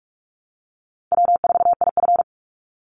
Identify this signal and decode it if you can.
Morse code, heard as a keyed tone, about 44 wpm, 700 Hz. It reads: W4IF
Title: Easy Call Signs Repeated 3x - 45wpm